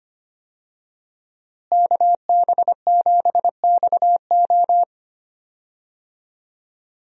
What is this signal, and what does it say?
Morse code, heard as a keyed tone, about 25 wpm, 700 Hz. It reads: KB7XO
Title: USA Call Signs - 5 Characters - Repeated 3x - 25wpm